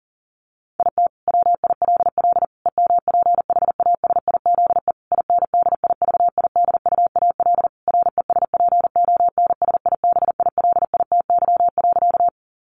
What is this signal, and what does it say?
Morse code, heard as a keyed tone, about 40 wpm, 700 Hz. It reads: IT WILL EMPHASIZE INDIVIDUAL RESPONSIBILITY.
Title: Sentences from Top 1500 Words 40wpm